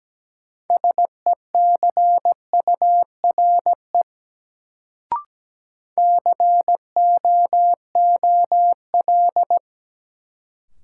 Morse code, heard as a keyed tone, about 17 wpm, 700 Hz. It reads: SECURE COOL
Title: Sentences from Top 1200 Words - New Word Drill Alt - 17wpm